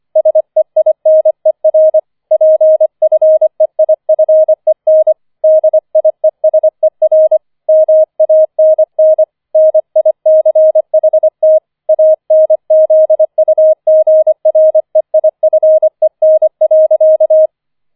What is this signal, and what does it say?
Morse code, heard as a keyed tone, about 25 wpm, 630 Hz. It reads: SEINER PFEIFEN DIESER MANN NICHT ANZUGREIFEN.
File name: 004 Vierter Streich 25wpm